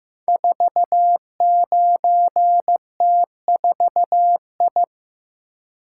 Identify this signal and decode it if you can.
Morse code, heard as a keyed tone, about 15 wpm, 700 Hz. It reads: 49T4I
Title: Code Groups - 5 Letter:Numbers - 15wpm